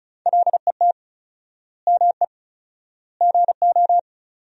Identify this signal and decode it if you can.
Morse code, heard as a keyed tone, about 34 wpm, 700 Hz. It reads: LET ME GO
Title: 4x Word-Spacing - Sentences from Top 300 Words - 35wpm